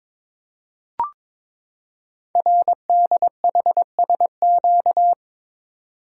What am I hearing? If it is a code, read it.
Morse code, heard as a keyed tone, about 22 wpm, 700 Hz. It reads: RDHSQ
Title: Code Groups - 5 Letters - 22wpm